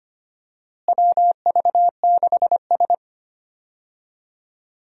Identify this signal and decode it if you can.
Morse code, heard as a keyed tone, about 25 wpm, 700 Hz. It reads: WV6S
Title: USA Call Signs - 4 Characters - Repeated 3x - 25wpm